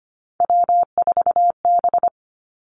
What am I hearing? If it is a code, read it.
Morse code, heard as a keyed tone, about 25 wpm, 700 Hz. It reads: W4B
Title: Three Challenging Letter-Number Combinations 25wpm